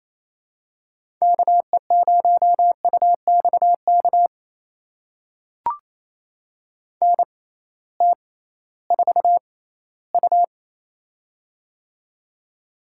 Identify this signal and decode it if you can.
Morse code, heard as a keyed tone, about 28 wpm, spaced as for 16 wpm, 700 Hz. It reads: KE0UXK NT4U
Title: USA Call Signs - 6x Character Spacing - 28wpm